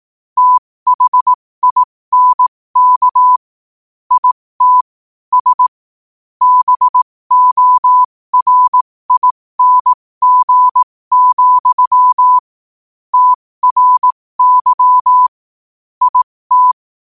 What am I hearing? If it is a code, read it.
Morse code, heard as a keyed tone, about 18 wpm, spaced as for 15 wpm, 1000 Hz. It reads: THINK IT S BORING, TRY IT
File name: qotdfiles / 15wpm